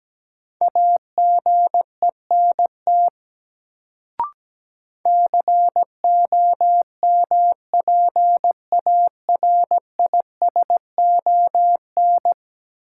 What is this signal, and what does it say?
Morse code, heard as a keyed tone, about 17 wpm, 700 Hz. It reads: AGENT COMPARISON
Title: Sentences from Top 1600 Words - New Word Drill - 17wpm